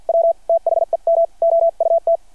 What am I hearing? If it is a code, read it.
Morse code, heard as a keyed tone, about 48 wpm, 660 Hz. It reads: W THEM OUT
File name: archived / CRSnet